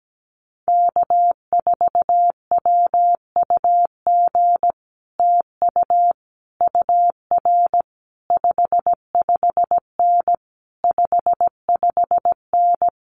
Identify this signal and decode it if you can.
Morse code, heard as a keyed tone, about 17 wpm, 700 Hz. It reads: K4WUG TU UR 55N 55N
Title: Common QSO Phrases Encore 17wpm